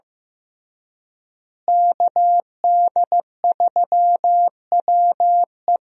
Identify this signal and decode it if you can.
Morse code, heard as a keyed tone, about 15 wpm, 700 Hz. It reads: KD3WE
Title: USA Call Signs - 5 Characters - Repeated 3x - 15wpm